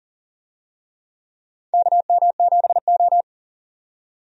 Morse code, heard as a keyed tone, about 40 wpm, 700 Hz. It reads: KM7O
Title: USA Call Signs - 4 Characters - 40wpm